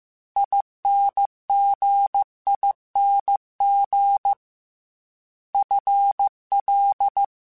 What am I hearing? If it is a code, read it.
Morse code, heard as a keyed tone, about 15 wpm, 800 Hz. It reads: INGING FL
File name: Special INGs / Special INGs-mp3s / Dble Ings sp @13